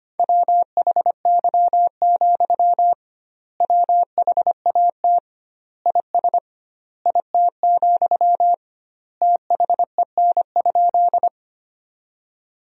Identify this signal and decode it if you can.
Morse code, heard as a keyed tone, about 25 wpm, 700 Hz. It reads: WHY, WHAT IS IT, THEN?
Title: 2x Word-Spacing - Sentences from Top 200 Words - 25wpm